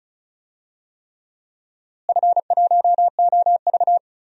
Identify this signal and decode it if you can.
Morse code, heard as a keyed tone, about 36 wpm, 700 Hz. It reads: F1OV
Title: DXCC-100 Call Signs - 4 Characters - Repeated 3x - 35wpm